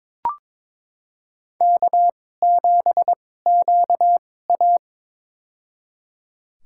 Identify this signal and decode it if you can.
Morse code, heard as a keyed tone, about 22 wpm, spaced as for 16 wpm, 700 Hz. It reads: K7QA
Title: USA Call Signs - 2x Character Spacing - 22wpm